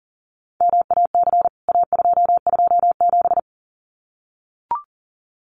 Morse code, heard as a keyed tone, about 38 wpm, 700 Hz. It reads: MAC A227
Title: CWT Exchange 40wpm